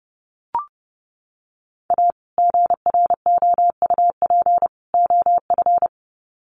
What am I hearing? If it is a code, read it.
Morse code, heard as a keyed tone, about 30 wpm, 700 Hz. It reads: A GROUP OF
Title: Sets of 3 Words 30wpm